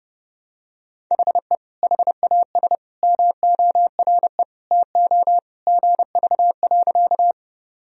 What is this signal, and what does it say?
Morse code, heard as a keyed tone, about 32 wpm, 700 Hz. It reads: HE HAS MORE TO GV.
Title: Sentences from Top 200 Words and Abbreviations 30wpm